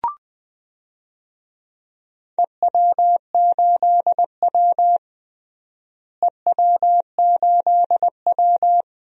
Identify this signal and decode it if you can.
Morse code, heard as a keyed tone, about 20 wpm, 700 Hz. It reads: EW8W EW8W
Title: DXCC-100 Call Signs - 4 Characters - Repeated 3x - 20wpm